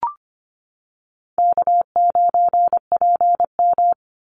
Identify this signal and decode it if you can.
Morse code, heard as a keyed tone, about 25 wpm, 690 Hz. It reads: K9PM
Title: NA SOTA Activator Callsigns 25wpm